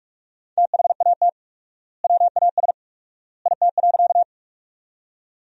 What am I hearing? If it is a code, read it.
Morse code, heard as a keyed tone, about 46 wpm, 700 Hz. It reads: THAT WAS IT.
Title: 4x Word-Spacing - Sentences from Top 100 Words - 45wpm